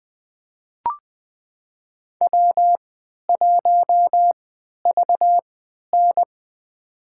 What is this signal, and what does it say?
Morse code, heard as a keyed tone, about 20 wpm, spaced as for 11 wpm, 700 Hz. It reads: W1VN
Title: USA Call Signs - 3x Character Spacing - 20wpm